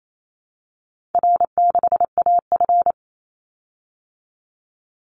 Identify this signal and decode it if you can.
Morse code, heard as a keyed tone, about 28 wpm, 700 Hz. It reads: R6AF
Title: Easy Call Signs Repeated 3x - 28wpm